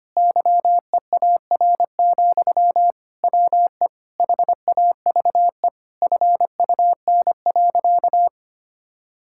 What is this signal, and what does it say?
Morse code, heard as a keyed tone, about 25 wpm, 700 Hz. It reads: YEAR, WE HAVE FUN.
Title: Sentences from Top 1000 Words - Review Part 5 - 25wpm